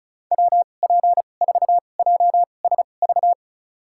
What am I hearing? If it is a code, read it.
Morse code, heard as a keyed tone, about 34 wpm, spaced as for 25 wpm, 700 Hz. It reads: WP4JSV
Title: USA Call Signs - 6 Characters - 2x Character Spacing - 35wpm